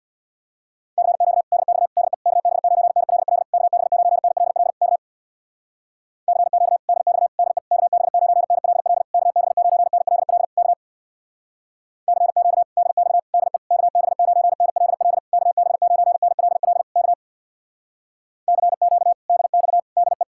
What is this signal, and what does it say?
Morse code, heard as a keyed tone, about 67 wpm, 700 Hz. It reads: CQ DX DE KB9NLL KB9NLL K CQ DX DE KB9NLL KB9NLL K CQ DX DE KB9NLL KB9NLL K CQ DX DE